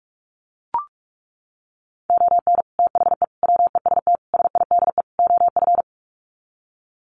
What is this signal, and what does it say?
Morse code, heard as a keyed tone, about 44 wpm, 700 Hz. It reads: ON THE WEST SIDE OF
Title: Sets of 5 Words Encore2 45wpm